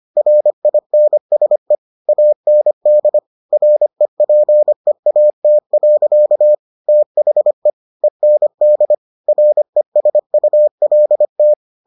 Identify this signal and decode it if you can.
Morse code, heard as a keyed tone, about 25 wpm, 600 Hz. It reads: RINSE AND REPEAT. THE END RESULT